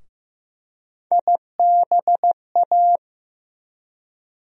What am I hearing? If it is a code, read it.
Morse code, heard as a keyed tone, about 15 wpm, 700 Hz. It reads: IBA
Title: USA Call Sign Postfixes - 3 Letters 15wpm